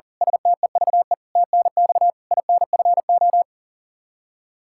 Morse code, heard as a keyed tone, about 40 wpm, 700 Hz. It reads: STEVE TNX INFO
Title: Practice QSO - Exchange 3 - More Info - 40wpm